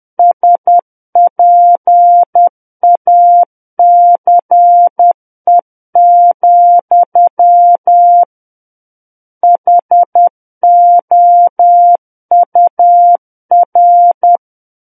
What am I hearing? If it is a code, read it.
Morse code, heard as a keyed tone, about 10 wpm, 700 Hz. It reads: SPACE, HOUR
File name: Book One / 01-Chapter 1 - 10 WPM